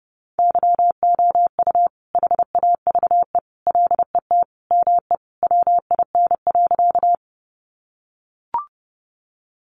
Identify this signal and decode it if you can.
Morse code, heard as a keyed tone, about 30 wpm, 700 Hz. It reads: YOU HAVE LET ME WIN.
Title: Sentences from Top 1500 Words 30wpm